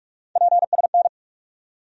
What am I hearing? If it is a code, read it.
Morse code, heard as a keyed tone, about 46 wpm, 700 Hz. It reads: PSN
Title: USA Call Sign Postfixes - 3 Letters 45wpm